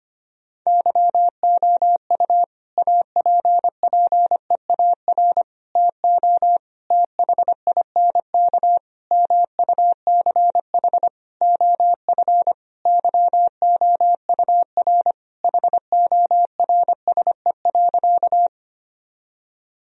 Morse code, heard as a keyed tone, about 25 wpm, 700 Hz. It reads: YOU APPEAR TO THINK MUCH OF YOUR HORSE.
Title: Sentences from Top 1000 Words - Review Part 2 - 25wpm